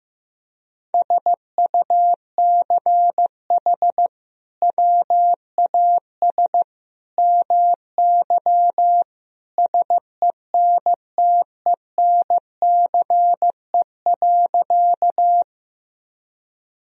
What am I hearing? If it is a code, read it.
Morse code, heard as a keyed tone, about 16 wpm, 700 Hz. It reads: SUCH WAS MY SENTENCE.
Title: Sentences from Top 1000 Words - Review Part 1 - 15wpm